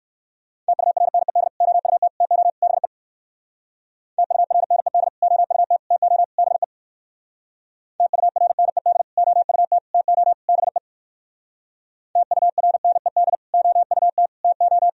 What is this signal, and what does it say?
Morse code, heard as a keyed tone, about 58 wpm, 700 Hz. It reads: TURNED OUT TO BE TURNED OUT TO BE TURNED OUT TO BE TURNED OUT TO